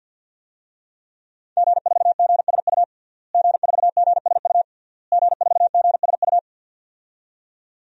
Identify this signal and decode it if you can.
Morse code, heard as a keyed tone, about 50 wpm, 700 Hz. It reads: G4GSU G4GSU G4GSU
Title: DXCC-100 Call Signs - 5 Characters - Repeated 3x - 50wpm